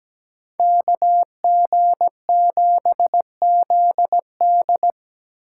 Morse code, heard as a keyed tone, about 20 wpm, 700 Hz. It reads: KG7ZD
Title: Speed Racing - USA Call Signs 17wpm